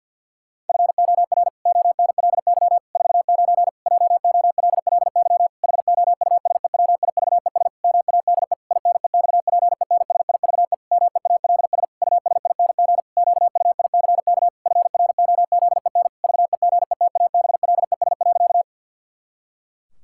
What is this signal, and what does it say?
Morse code, heard as a keyed tone, about 51 wpm, 700 Hz. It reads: FOR ONLY 49 JOLLY HOUSEWIVES MADE INEXPENSIVE MEALS USING QUICK FROZEN VEGETABLES.
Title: Pangram Challenge v2 - 50wpm